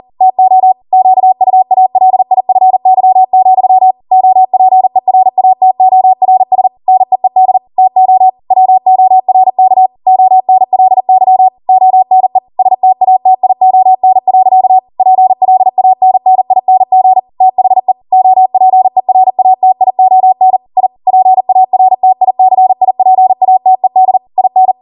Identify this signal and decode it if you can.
Morse code, heard as a keyed tone, about 40 wpm, 750 Hz. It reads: TO QUALIFY, OPERATORS NEED TO WORK ONLY ONE STATION. PLANNING THE OPERATION I PARTICIPATED IN